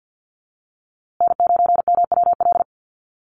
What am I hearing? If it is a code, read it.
Morse code, heard as a keyed tone, about 49 wpm, 700 Hz. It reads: N9MWL
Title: USA Call Signs Repeated 2x 50wpm